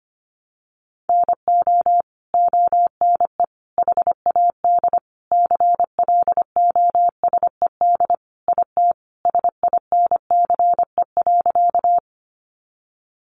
Morse code, heard as a keyed tone, about 25 wpm, 700 Hz. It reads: NO ONE HAD CLOSED IT SINCE.
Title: Sentences from Top 1400 Words 25wpm